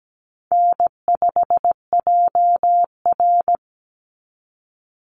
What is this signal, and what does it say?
Morse code, heard as a keyed tone, about 17 wpm, 700 Hz. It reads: N5JR
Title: Call Signs Repeated 2x - 17wpm